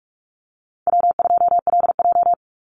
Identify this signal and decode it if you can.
Morse code, heard as a keyed tone, about 44 wpm, 700 Hz. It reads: W2LJ
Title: Call Signs - 45wpm